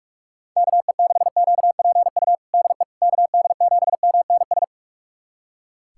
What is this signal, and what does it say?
Morse code, heard as a keyed tone, about 43 wpm, 700 Hz. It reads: KE6QPU DE KD7MNS
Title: USA Callsign DE Callsign 45wpm